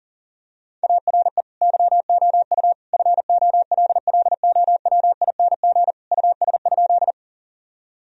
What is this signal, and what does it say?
Morse code, heard as a keyed tone, about 40 wpm, 700 Hz. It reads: ARE YOU FOLLOWING US?
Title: Sentences from Top 1000 Words - Review Part 4 - 40wpm